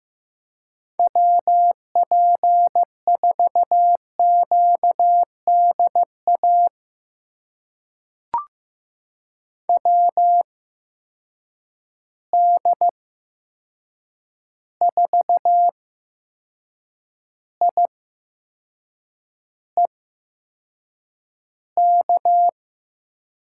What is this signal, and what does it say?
Morse code, heard as a keyed tone, about 15 wpm, spaced as for 6 wpm, 700 Hz. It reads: WP4QDA WD4IEK
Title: USA Call Signs - 6 Characters - 8x Character Spacing - 15wpm